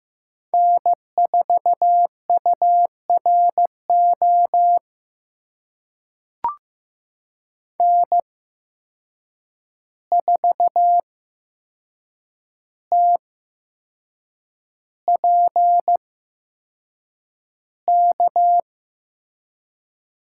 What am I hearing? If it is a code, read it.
Morse code, heard as a keyed tone, about 15 wpm, spaced as for 6 wpm, 700 Hz. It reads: N4URO N4TPK
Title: USA Call Signs - 5 Characters - 8x Character Spacing - 15wpm